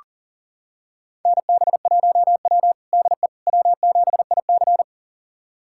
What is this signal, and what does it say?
Morse code, heard as a keyed tone, about 37 wpm, 700 Hz. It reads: NB1W DE W7IC
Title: USA Callsign DE Callsign - 4 Characters - 40wpm